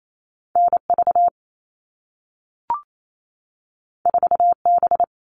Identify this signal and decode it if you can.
Morse code, heard as a keyed tone, about 28 wpm, 700 Hz. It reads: NV 4B
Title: All Cut Numbers - 2-Digit 28wpm